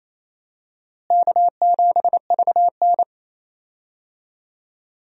Morse code, heard as a keyed tone, about 28 wpm, 700 Hz. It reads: K7VN
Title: USA Call Signs - 4 Characters - Repeated 3x - 28wpm